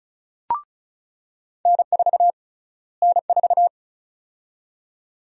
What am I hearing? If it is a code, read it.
Morse code, heard as a keyed tone, about 34 wpm, 700 Hz. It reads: N4 N4
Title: USA Call Sign Prefixes - Repeated 2x - 35wpm